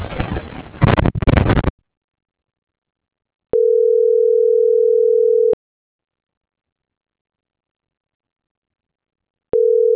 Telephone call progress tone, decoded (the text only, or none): ringback tone